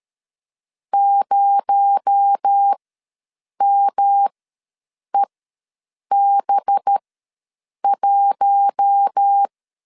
Morse code, heard as a keyed tone, about 13 wpm, spaced as for 7 wpm, 800 Hz. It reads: 0MEB1
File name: LEZIONI / 7 GF9Z / 7 GF ws 12wpm 3140